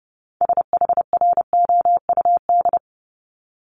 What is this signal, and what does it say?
Morse code, heard as a keyed tone, about 30 wpm, 700 Hz. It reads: SHROUD
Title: Multi-letter Phonemes 30wpm